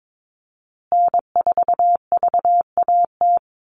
Morse code, heard as a keyed tone, about 22 wpm, 700 Hz. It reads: N4VAT
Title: USA Call Signs Repeated 3x 22wpm